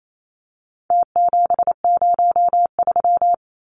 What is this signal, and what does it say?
Morse code, heard as a keyed tone, about 28 wpm, 700 Hz. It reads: T703
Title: Numbers - Contest Style - 4-Digit 28wpm